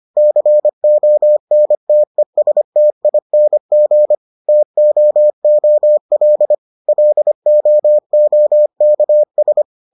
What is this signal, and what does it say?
Morse code, heard as a keyed tone, about 25 wpm, 600 Hz. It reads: CONTESTING TOOL LOOKS